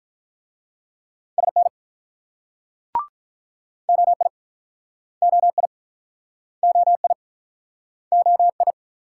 Morse code, heard as a keyed tone, about 44 wpm, 700 Hz. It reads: SN OI OI OI OI